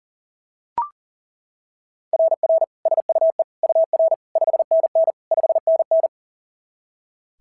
Morse code, heard as a keyed tone, about 40 wpm, 690 Hz. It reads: RR SUE UR 5NN 5NN
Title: SOTA QSO Practice 40wpm